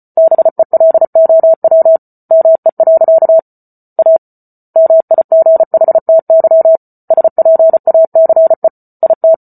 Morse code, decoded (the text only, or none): BELOW ME. A MIGHTY SPACE IT